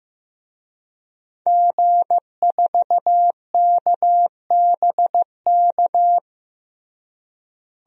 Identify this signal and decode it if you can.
Morse code, heard as a keyed tone, about 15 wpm, 700 Hz. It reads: G4KBK
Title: DXCC-100 Call Signs - 5 Characters - Repeated 2x - 15wpm